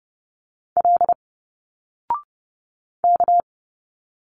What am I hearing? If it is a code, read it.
Morse code, heard as a keyed tone, about 30 wpm, 700 Hz. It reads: L K
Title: Single Letters 30wpm